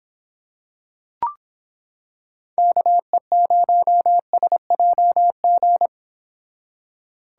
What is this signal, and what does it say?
Morse code, heard as a keyed tone, about 25 wpm, 700 Hz. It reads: KE0SJG